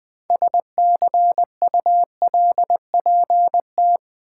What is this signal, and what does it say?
Morse code, heard as a keyed tone, about 20 wpm, 700 Hz. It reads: SCULPT
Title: Build A Word - 3 Letters At A Time - 9-Letter Words - 20wpm